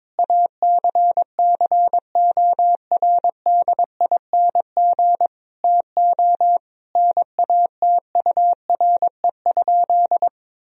Morse code, heard as a keyed tone, about 22 wpm, 700 Hz. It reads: ACCORDING TO NATURE?
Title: Sentences from Top 1000 Words - Review Part 4 - 22wpm